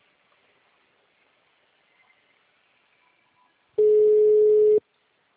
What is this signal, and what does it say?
Ringback tone (Europe, 425 Hz).
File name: out- -2040- - - .1199